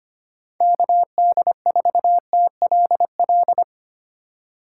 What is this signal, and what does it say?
Morse code, heard as a keyed tone, about 25 wpm, 700 Hz. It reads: KD4TLL
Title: USA Call Signs - 3x Character Spacing - 25wpm